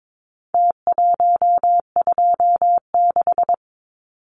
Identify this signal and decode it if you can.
Morse code, heard as a keyed tone, about 22 wpm, 700 Hz. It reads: T126
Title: Numbers - Contest Style - 4-Digit 22wpm